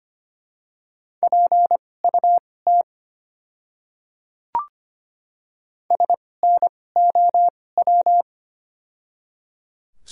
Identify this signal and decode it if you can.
Morse code, heard as a keyed tone, about 25 wpm, spaced as for 18 wpm, 700 Hz. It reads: PUT SNOW
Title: 2x Character Spacing - Top 500 Words - 25wpm